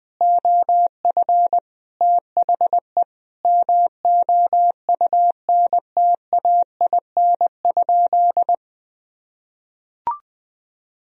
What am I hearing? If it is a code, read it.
Morse code, heard as a keyed tone, about 19 wpm, 700 Hz. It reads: OF THE MOUNTAIN?
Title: Sentences from Top 1000 Words - Review Part 1 - 20wpm